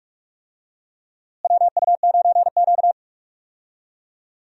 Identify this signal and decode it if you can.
Morse code, heard as a keyed tone, about 43 wpm, 700 Hz. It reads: WU9Q
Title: USA Call Signs - 4 Characters - Repeated 2x - 45wpm